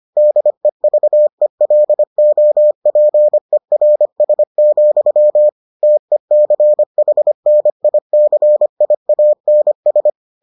Morse code, heard as a keyed tone, about 25 wpm, 600 Hz. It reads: DEVELOPERS, TECHNICIANS